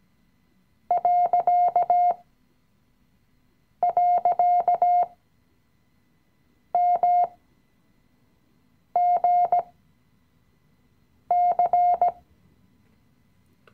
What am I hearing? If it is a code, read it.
Morse code, heard as a keyed tone, about 17 wpm, spaced as for 5 wpm, 710 Hz. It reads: ..MGC
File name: CW-Online-Kurs / Anfänger / Lektion 018